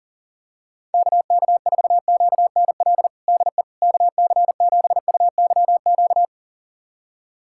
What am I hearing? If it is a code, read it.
Morse code, heard as a keyed tone, about 40 wpm, 700 Hz. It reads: KK4QNL DE KC7UYQ
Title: USA Callsign DE Callsign - 6 Characters - 40wpm